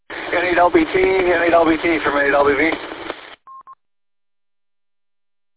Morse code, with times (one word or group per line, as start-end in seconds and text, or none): none